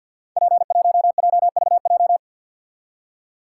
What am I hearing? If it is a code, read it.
Morse code, heard as a keyed tone, about 50 wpm, 700 Hz. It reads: P1JVJ
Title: Code Groups - 5 Letter:Numbers - 50wpm